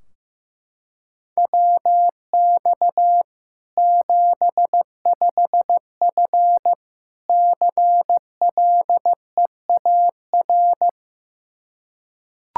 Morse code, heard as a keyed tone, about 16 wpm, 700 Hz. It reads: WX 75F CLEAR
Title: K1USN SST QSO Practice 15wpm